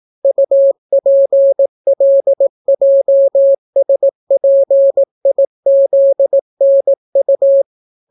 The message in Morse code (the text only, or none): UPLJSPIZNU